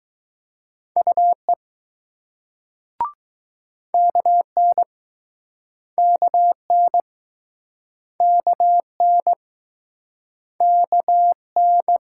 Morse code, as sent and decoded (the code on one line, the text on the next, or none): ..- . / -.- -. / -.- -. / -.- -. / -.- -.
UE KN KN KN KN